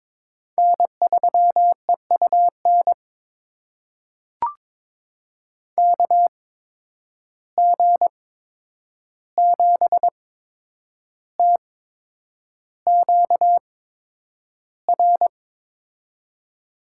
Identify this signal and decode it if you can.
Morse code, heard as a keyed tone, about 22 wpm, spaced as for 9 wpm, 700 Hz. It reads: N3EUN KG7TQR
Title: USA Call Signs - 8x Character Spacing - 22wpm